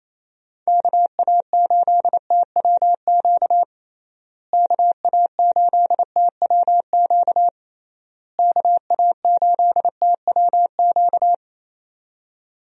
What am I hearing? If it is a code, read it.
Morse code, heard as a keyed tone, about 27 wpm, 700 Hz. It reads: KA8TWQ KA8TWQ KA8TWQ
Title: USA Call Signs - 6 Characters - Repeated 3x - 28wpm